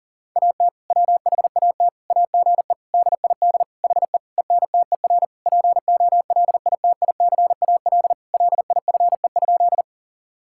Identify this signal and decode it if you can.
Morse code, heard as a keyed tone, about 40 wpm, 700 Hz. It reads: AT WHAT AGE DID HE ENTER POLITICAL LIFE?
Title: Sentences from Top 1000 Words - Review Part 5 - 40wpm